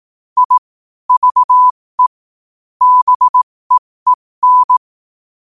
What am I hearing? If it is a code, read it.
Morse code, heard as a keyed tone, about 18 wpm, spaced as for 15 wpm, 1000 Hz. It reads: I VE BEEN
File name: qotdfiles / 15wpm